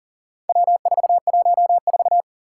Morse code, heard as a keyed tone, about 40 wpm, 700 Hz. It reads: W414
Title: Four Challenging Letter-Number Combinations 40wpm